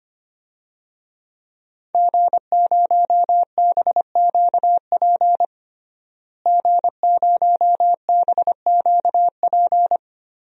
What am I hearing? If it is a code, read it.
Morse code, heard as a keyed tone, about 25 wpm, 700 Hz. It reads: G0BQP G0BQP
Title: DXCC-100 Call Signs - 5 Characters - Repeated 3x - 25wpm